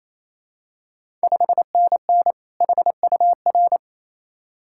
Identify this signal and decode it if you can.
Morse code, heard as a keyed tone, about 28 wpm, 700 Hz. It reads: 5NN HUR
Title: Ontario QSO Party Exchange 28wpm